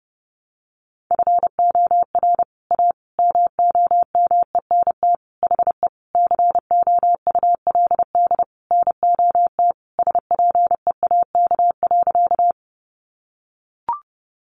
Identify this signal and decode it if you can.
Morse code, heard as a keyed tone, about 30 wpm, 700 Hz. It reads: FOR A MOMENT HE COULD NOT SPEAK.
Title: Sentences from Top 1100 Words 30wpm